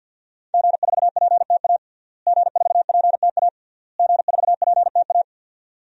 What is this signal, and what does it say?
Morse code, heard as a keyed tone, about 47 wpm, 700 Hz. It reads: G4PTA G4PTA G4PTA
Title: DXCC-100 Call Signs - 5 Characters - Repeated 3x - 50wpm